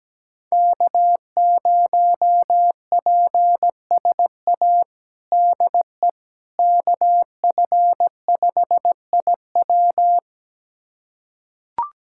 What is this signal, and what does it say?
Morse code, heard as a keyed tone, about 17 wpm, 700 Hz. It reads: K0PSA DE KF5IW
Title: USA Callsign DE Callsign - 5 Characters - 17wpm